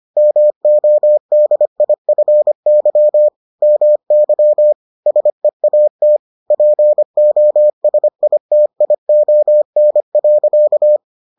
Morse code, heard as a keyed tone, about 25 wpm, 600 Hz. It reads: MODIFY MY SEAT POSITION.